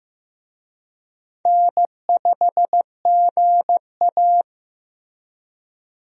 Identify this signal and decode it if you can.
Morse code, heard as a keyed tone, about 15 wpm, 700 Hz. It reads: N5GA
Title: USA Call Signs - 4 Characters - Repeated 3x - 15wpm